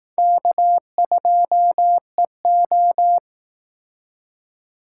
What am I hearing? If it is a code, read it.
Morse code, heard as a keyed tone, about 18 wpm, 700 Hz. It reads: K2EO
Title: Speed Racing - USA 1x2 Call Signs 15wpm